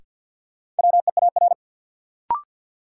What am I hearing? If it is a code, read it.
Morse code, heard as a keyed tone, about 50 wpm, 700 Hz. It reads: WEAR
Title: Sentences from Top 1500 Words - New Word Drill Alt - 50wpm